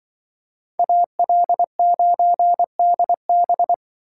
Morse code, heard as a keyed tone, about 24 wpm, 700 Hz. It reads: AL9DB
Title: Speed Racing - USA Call Signs 20wpm